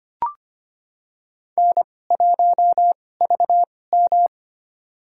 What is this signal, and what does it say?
Morse code, heard as a keyed tone, about 25 wpm, spaced as for 18 wpm, 700 Hz. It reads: N1VM
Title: USA Call Signs - 4 Characters - 2x Character Spacing - 25wpm